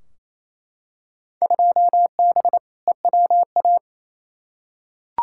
Morse code, heard as a keyed tone, about 28 wpm, 700 Hz. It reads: 2B EWA
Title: Field Day Exchanges Encore - 28wpm